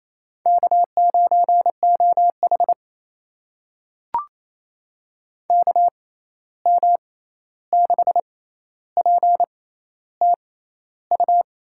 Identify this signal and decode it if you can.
Morse code, heard as a keyed tone, about 28 wpm, spaced as for 13 wpm, 700 Hz. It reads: K9OH KM6PTU
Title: USA Call Signs - 6x Character Spacing - 28wpm